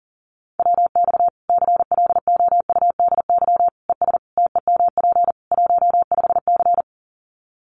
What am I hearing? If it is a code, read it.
Morse code, heard as a keyed tone, about 37 wpm, 700 Hz. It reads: WX CLOUDY ES TEMP 15C
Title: Common QSO Phrases 40wpm